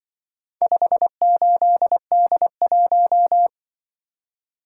Morse code, heard as a keyed tone, about 24 wpm, 700 Hz. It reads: 58D1
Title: Speed-Racing - Four Letter-Number Combinations - 20wpm